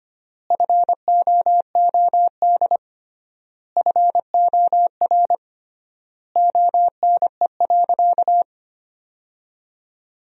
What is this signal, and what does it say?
Morse code, heard as a keyed tone, about 25 wpm, 700 Hz. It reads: FOOD FOR ONE.
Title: 3x Word-Spacing - Sentences from Top 300 Words - 25wpm